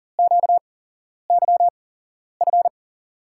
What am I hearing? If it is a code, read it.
Morse code, heard as a keyed tone, about 40 wpm, spaced as for 11 wpm, 700 Hz. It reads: QYF
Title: USA Call Signs - 6 Characters - 8x Character Spacing - 40wpm